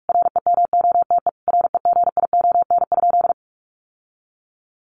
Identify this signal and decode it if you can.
Morse code, heard as a keyed tone, about 44 wpm, 700 Hz. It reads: REMOTE REGION?
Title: Sentences from Top 1500 Words 45wpm